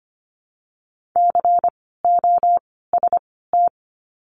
Morse code, heard as a keyed tone, about 25 wpm, 700 Hz. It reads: C O S T
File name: course / 25:10wpm / 094 Lesson 16f - Letter c - Words 25wpm